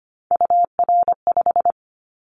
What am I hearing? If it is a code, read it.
Morse code, heard as a keyed tone, about 25 wpm, 700 Hz. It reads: UR5
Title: Callsign Prefixes for Top 25 DXCC - 25wpm